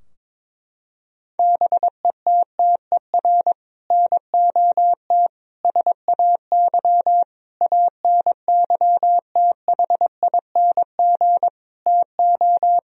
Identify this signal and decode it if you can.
Morse code, heard as a keyed tone, about 22 wpm, 700 Hz. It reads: BETTER NOT SAY ANYTHING TO
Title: Sentences from Top 1000 Words - Review Part 4 - 22wpm